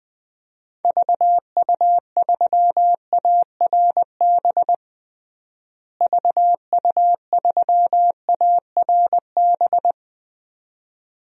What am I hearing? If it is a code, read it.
Morse code, heard as a keyed tone, about 21 wpm, 700 Hz. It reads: VU3ARB VU3ARB
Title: DXCC-100 Call Signs - 6 Characters - Repeated 2x - 20wpm